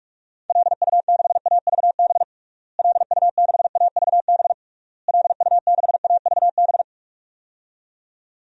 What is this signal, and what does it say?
Morse code, heard as a keyed tone, about 45 wpm, 700 Hz. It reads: LU6AVB LU6AVB LU6AVB
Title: DXCC-100 Call Signs - 6 Characters - Repeated 3x - 45wpm